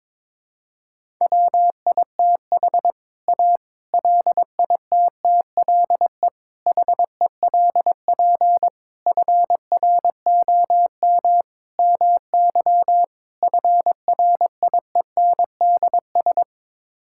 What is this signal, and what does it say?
Morse code, heard as a keyed tone, about 22 wpm, 700 Hz. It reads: WITH A LITTLE HELP FROM MY FRIENDS
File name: practice / 22wpm / Sets of 7 Words - 22wpm